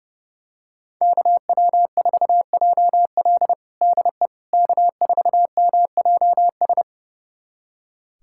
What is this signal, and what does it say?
Morse code, heard as a keyed tone, about 32 wpm, 700 Hz. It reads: KW4JL DE K4MJS
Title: USA Callsign DE Callsign - 5 Characters - 30wpm